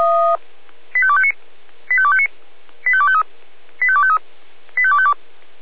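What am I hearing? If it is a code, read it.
Five-tone selective call (ZVEI), heard as quick runs of five tones, tones of about 70 ms. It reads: 85259 85259 85253 85253 85252
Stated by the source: Five-tone ZVEI selective call received by SDR.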